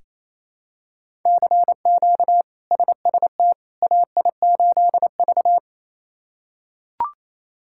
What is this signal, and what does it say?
Morse code, heard as a keyed tone, about 28 wpm, 700 Hz. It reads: CQ SST AI8V
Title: K1USN SST QSO Practice - 28wpm